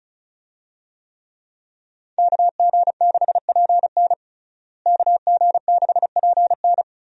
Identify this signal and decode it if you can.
Morse code, heard as a keyed tone, about 36 wpm, 700 Hz. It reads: KG6PN KG6PN
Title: USA Call Signs - 5 Characters - Repeated 3x - 35wpm